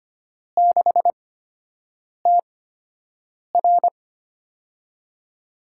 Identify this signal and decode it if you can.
Morse code, heard as a keyed tone, about 25 wpm, spaced as for 7 wpm, 700 Hz. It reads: 6TR
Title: USA Call Signs - 4 Characters - 8x Character Spacing - 25wpm